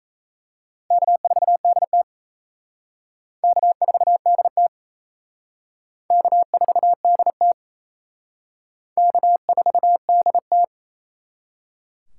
Morse code, heard as a keyed tone, about 36 wpm, 700 Hz. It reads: K4DT K4DT K4DT K4DT